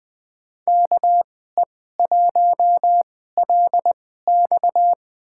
Morse code, heard as a keyed tone, about 20 wpm, spaced as for 14 wpm, 700 Hz. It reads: KE1LX
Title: USA Call Signs - 2x Character Spacing - 20wpm